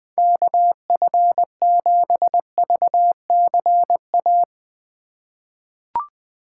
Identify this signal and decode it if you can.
Morse code, heard as a keyed tone, about 20 wpm, 700 Hz. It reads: KF7VCA
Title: USA Call Signs - 6 Characters - 2x Character Spacing - 20wpm